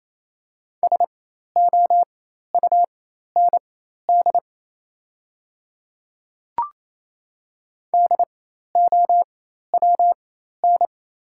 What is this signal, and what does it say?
Morse code, heard as a keyed tone, about 28 wpm, spaced as for 13 wpm, 700 Hz. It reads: SOUND DOWN
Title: 4x Character Spacing - Top 100 Words - 28wpm